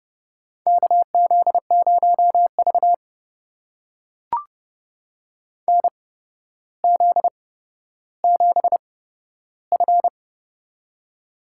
Morse code, heard as a keyed tone, about 30 wpm, spaced as for 13 wpm, 700 Hz. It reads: KZ0V NZ7F
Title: USA Call Signs - 4 Characters - 8x Character Spacing - 30wpm